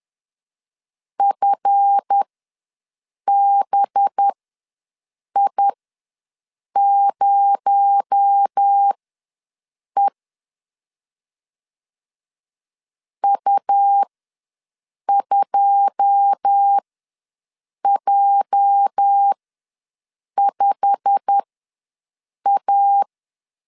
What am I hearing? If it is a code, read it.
Morse code, heard as a keyed tone, about 10 wpm, spaced as for 6 wpm, 800 Hz. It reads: FBI0E U2J5A
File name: LEZIONI / 9 2 ws 10wpm 4130